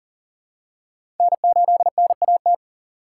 Morse code, heard as a keyed tone, about 40 wpm, 700 Hz. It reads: N8NAT
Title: USA Call Signs - 5 Characters - 2x Character Spacing - 40wpm